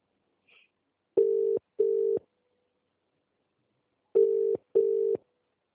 Ringback tone (UK), 400 + 450 Hz.